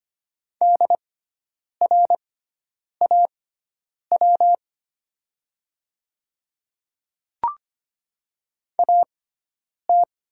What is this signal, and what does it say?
Morse code, heard as a keyed tone, about 25 wpm, spaced as for 9 wpm, 700 Hz. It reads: DRAW AT
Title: 6x Character Spacing - Top 500 Words - 25wpm